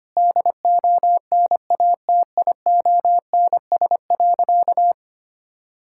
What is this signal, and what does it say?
Morse code, heard as a keyed tone, about 25 wpm, 700 Hz. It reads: DONATIONS.
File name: practice / 25wpm / Sentences with 9-Word Phrases - 25wpm